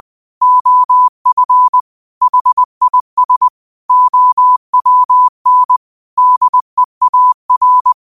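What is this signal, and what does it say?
Morse code, heard as a keyed tone, about 20 wpm, 1000 Hz. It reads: OF HIS OWN DEAR
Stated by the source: Morse code quote of the day at 20 wpm.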